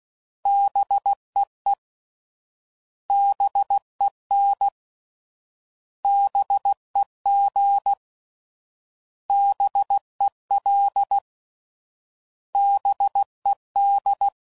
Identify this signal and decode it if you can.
Morse code, heard as a keyed tone, about 16 wpm, 800 Hz. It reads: BEE BEN BEG BEL BED